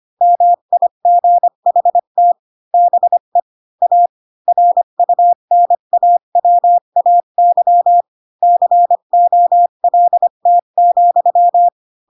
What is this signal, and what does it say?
Morse code, heard as a keyed tone, about 25 wpm, 700 Hz. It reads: MIGHT BE A RUNAWAY COLT,
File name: Pinocchio / Pinocchio ch3 20 25 ik1hge